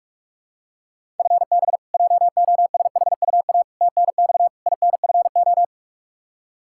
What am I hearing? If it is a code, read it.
Morse code, heard as a keyed tone, about 46 wpm, 700 Hz. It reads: FB JOSHUA TNX INFO
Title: K1USN SST QSO Practice 45wpm